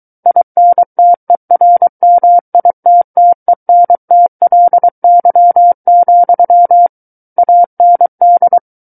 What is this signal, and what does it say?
Morse code, heard as a keyed tone, about 23 wpm, 700 Hz. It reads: INTERMITTENTLY, AND
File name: Book One / 14-Chapter 14 - 23 WPM